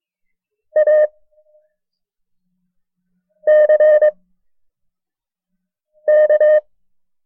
Morse code, heard as a keyed tone, about 22 wpm, 610 Hz. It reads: A C K
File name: FreeFiles / CW-practice / KB6MT-tape1-side2